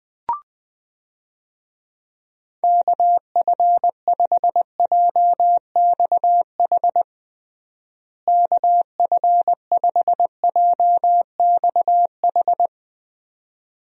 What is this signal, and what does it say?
Morse code, heard as a keyed tone, about 20 wpm, 700 Hz. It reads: KF5JXH KF5JXH
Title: USA Call Signs - 6 Characters - Repeated 2x - 20wpm